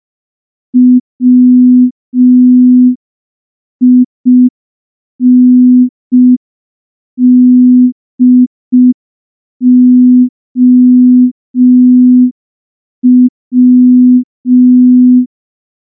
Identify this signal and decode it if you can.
Morse code, heard as a keyed tone, about 5 wpm, 250 Hz. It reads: WINDOW
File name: morse / morse-training-05WPM